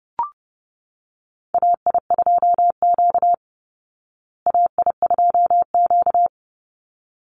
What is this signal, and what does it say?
Morse code, heard as a keyed tone, about 28 wpm, 700 Hz. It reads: AI2Q AI2Q
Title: Call Signs Repeated 3x - 30wpm